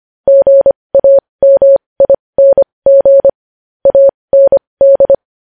Morse code, heard as a keyed tone, about 25 wpm, spaced as for 20 wpm, 550 Hz. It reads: GAMING AND
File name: podcast / News headlines-1x-25x20